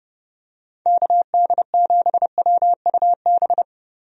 Morse code, heard as a keyed tone, about 30 wpm, 700 Hz. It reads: KD7WUB
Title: USA Call Signs - 6 Characters - Repeated 3x - 30wpm